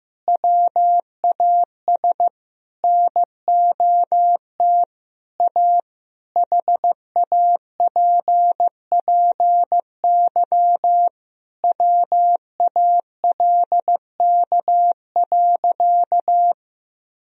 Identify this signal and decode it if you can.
Morse code, heard as a keyed tone, about 15 wpm, 700 Hz. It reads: WAS NOT A HAPPY WALK.
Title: Sentences from Top 1000 Words - Review Part 1 - 15wpm